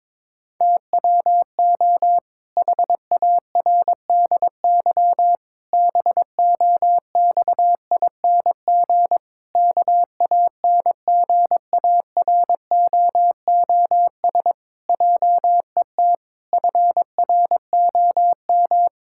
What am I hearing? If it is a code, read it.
Morse code, heard as a keyed tone, about 22 wpm, 700 Hz. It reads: TWO HARDY BOXING KANGAROOS JET FROM
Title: Pangram Challenge v2 - 22wpm